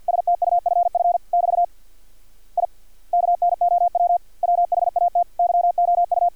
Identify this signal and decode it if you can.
Morse code, heard as a keyed tone, about 50 wpm, 700 Hz, with radio noise. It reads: STUFF <BT> I KNOW WHAT YOU
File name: archived / CRSnet